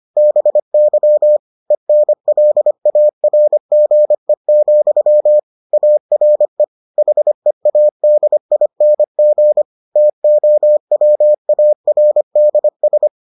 Morse code, heard as a keyed tone, about 25 wpm, 600 Hz. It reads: BY ENLARGE, ARE HEADING TOWARDS